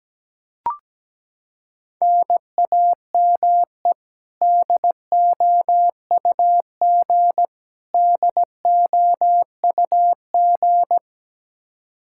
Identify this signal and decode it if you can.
Morse code, heard as a keyed tone, about 17 wpm, 700 Hz. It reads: NAME DOUG DOUG
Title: Practice QSO - Exchange 1 - Beginning - 17wpm